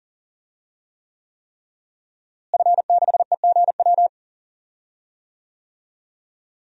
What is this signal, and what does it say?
Morse code, heard as a keyed tone, about 40 wpm, 700 Hz. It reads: F6EGW
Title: DXCC-100 Call Signs - 5 Characters - Repeated 3x - 40wpm